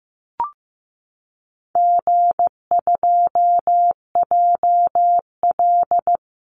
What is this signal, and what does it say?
Morse code, heard as a keyed tone, about 15 wpm, 700 Hz. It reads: G2JL
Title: Easy Call Signs Repeated 3x - 15wpm